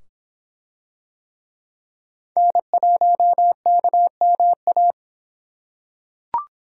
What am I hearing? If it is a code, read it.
Morse code, heard as a keyed tone, about 24 wpm, 700 Hz. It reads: N1KMA